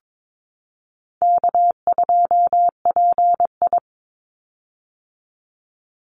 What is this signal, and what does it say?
Morse code, heard as a keyed tone, about 22 wpm, 700 Hz. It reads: K2PI
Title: Easy Call Signs - 22wpm